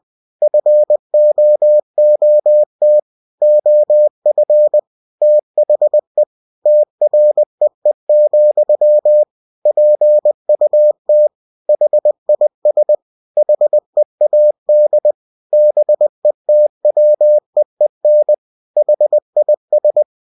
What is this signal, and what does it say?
Morse code, heard as a keyed tone, about 20 wpm, 600 Hz. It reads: FOOT OF THE TREE, PUT HIS HEAD BETWEEN HIS
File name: cw-books / winnie-the-pooh / winnie-20wpm-page-0002